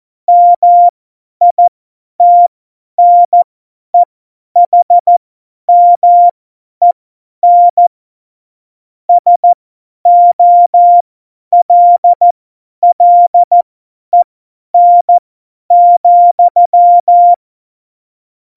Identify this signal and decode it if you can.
Morse code, heard as a keyed tone, about 14 wpm, spaced as for 10 wpm, 700 Hz. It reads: MITNEHMEN SOLLEN,
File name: Morsekurs / Witze / Witz 07 April 2020, 50 F70